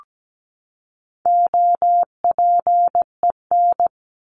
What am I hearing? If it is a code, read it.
Morse code, heard as a keyed tone, about 17 wpm, 700 Hz. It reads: OPEN
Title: Top 300 Words 17wpm